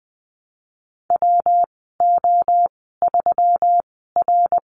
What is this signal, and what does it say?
Morse code, heard as a keyed tone, about 20 wpm, spaced as for 14 wpm, 700 Hz. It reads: WO3R
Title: 167 Lesson 27g - Number 7 - Call Signs 20wpm